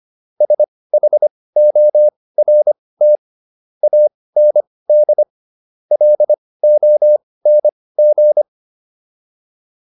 Morse code, heard as a keyed tone, about 25 wpm, spaced as for 18 wpm, 600 Hz. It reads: SHORT AND LONG